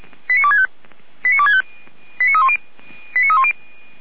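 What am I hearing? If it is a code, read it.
Five-tone selective call (ZVEI), heard as quick runs of five tones, tones of about 70 ms. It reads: 89265 89265 89310 89310
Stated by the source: ZVEI five-tone selective call received on SDR, feuerwehr-orlamuende.org fire brigade archive.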